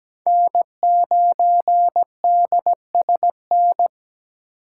Morse code, heard as a keyed tone, about 17 wpm, 700 Hz. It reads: N9DSN
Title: USA Call Signs - 5 Characters - 8x Character Spacing - 17wpm